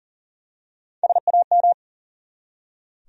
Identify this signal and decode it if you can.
Morse code, heard as a keyed tone, about 40 wpm, 700 Hz. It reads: SAM
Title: USA Common Names - Max 3 Letters - 40wpm